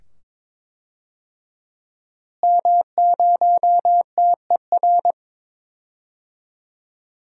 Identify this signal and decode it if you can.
Morse code, heard as a keyed tone, about 22 wpm, 700 Hz. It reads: M0TER
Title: DXCC-100 Call Signs - 5 Characters - Repeated 2x - 22wpm